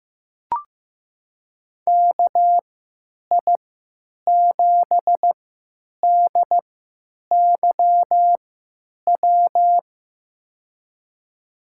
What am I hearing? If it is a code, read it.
Morse code, heard as a keyed tone, about 15 wpm, spaced as for 9 wpm, 700 Hz. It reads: KI7DYW
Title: USA Call Signs - 3x Character Spacing - 15wpm